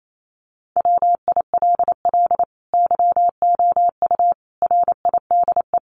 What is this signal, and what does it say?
Morse code, heard as a keyed tone, about 28 wpm, 700 Hz. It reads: WILL YOU RIDE
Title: Sentences from Top 1200 Words 28wpm